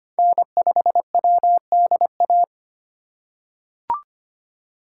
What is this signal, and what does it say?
Morse code, heard as a keyed tone, about 25 wpm, 700 Hz. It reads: N5WDA
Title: USA Call Signs - 4x Character Spacing - 25wpm